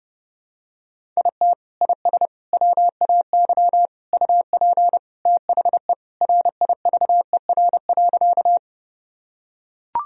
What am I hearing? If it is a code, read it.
Morse code, heard as a keyed tone, about 30 wpm, 700 Hz. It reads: IT IS WAY UP THE RIVER.
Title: Sentences from Top 1000 Words - Review Part 1 - 30wpm